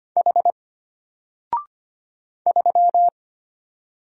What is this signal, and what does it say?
Morse code, heard as a keyed tone, about 25 wpm, 700 Hz. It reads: H 3
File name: Single-Letter-Number-25wpm